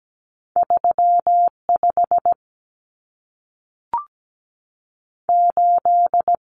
Morse code, heard as a keyed tone, about 17 wpm, 700 Hz. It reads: 35 8
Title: Common Cut Numbers - 2-Digit 17wpm